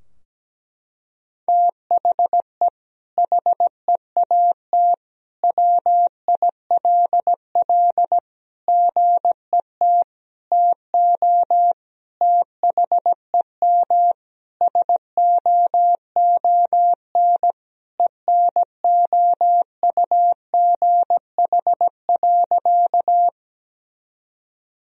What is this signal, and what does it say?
Morse code, heard as a keyed tone, about 17 wpm, 700 Hz. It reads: THE HEAT WILL GET TO THEM SOON ENOUGH.
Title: Sentences from Top 1000 Words - Review Part 3 - 17wpm